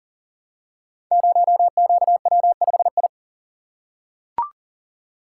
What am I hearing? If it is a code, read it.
Morse code, heard as a keyed tone, about 39 wpm, 700 Hz. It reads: 0QW5I
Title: Code Groups - 5 Letter:Numbers - 40wpm